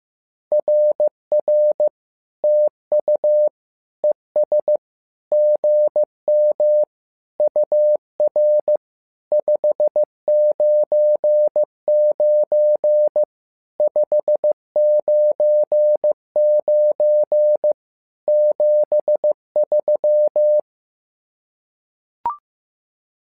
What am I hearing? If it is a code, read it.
Morse code, heard as a keyed tone, about 15 wpm, 600 Hz. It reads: RR TU ES GM UR 599 599 73
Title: SOTA QSO Practice 15wpm